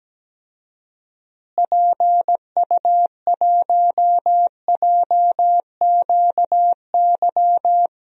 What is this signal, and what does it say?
Morse code, heard as a keyed tone, about 17 wpm, 700 Hz. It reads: PU1JQY
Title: DXCC-100 Call Signs - 6 Characters - Repeated 3x - 17wpm